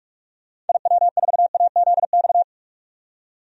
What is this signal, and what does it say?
Morse code, heard as a keyed tone, about 47 wpm, 690 Hz. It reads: IW4AZX
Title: DXCC-100 Call Signs - 6 Characters - 45wpm